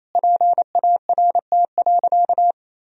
Morse code, heard as a keyed tone, about 28 wpm, 700 Hz. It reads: PART.
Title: Sentences from Top 1000 Words - Review Part 1 - 28wpm